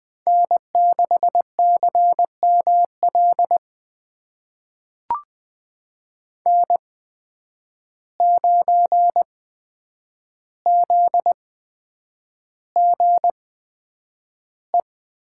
Morse code, heard as a keyed tone, about 20 wpm, spaced as for 9 wpm, 700 Hz. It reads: N6CML N9ZGE
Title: USA Call Signs - 5 Characters - 8x Character Spacing - 20wpm